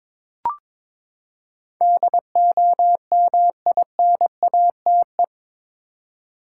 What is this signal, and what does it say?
Morse code, heard as a keyed tone, about 22 wpm, 700 Hz. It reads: DOMINATE
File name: practice / 22wpm / Eight Letter Words - 22wpm